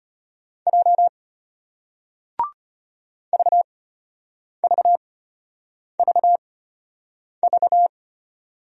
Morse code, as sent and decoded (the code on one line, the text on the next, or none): .--- / ...- ...- ...- ...-
J VVVV